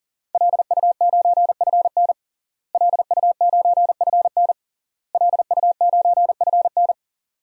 Morse code, heard as a keyed tone, about 37 wpm, 700 Hz. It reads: LU9FN LU9FN LU9FN
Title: DXCC-100 Call Signs - 5 Characters - Repeated 3x - 40wpm